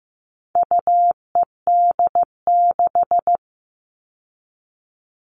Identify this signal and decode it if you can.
Morse code, heard as a keyed tone, about 15 wpm, 700 Hz. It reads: UED6
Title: All Cut Numbers - 4-Digit 15wpm